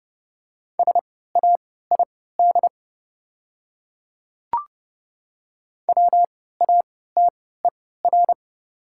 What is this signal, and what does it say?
Morse code, heard as a keyed tone, about 30 wpm, spaced as for 17 wpm, 690 Hz. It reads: SAID WATER
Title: 3x Character Spacing - Top 100 Words - 30wpm